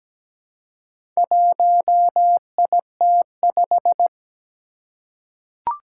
Morse code, heard as a keyed tone, about 17 wpm, 700 Hz. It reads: 1IT5
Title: Four Challenging Letter-Number Combinations 17wpm